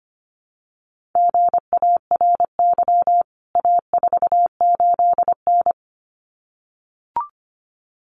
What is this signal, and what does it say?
Morse code, heard as a keyed tone, about 25 wpm, 700 Hz. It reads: GARY A48N
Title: CWT Exchange 25wpm